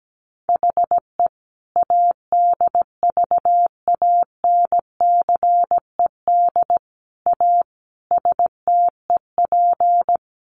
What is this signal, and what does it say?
Morse code, heard as a keyed tone, about 17 wpm, 700 Hz. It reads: HE ADVANCED A STEP
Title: Sentences from Top 1200 Words 17wpm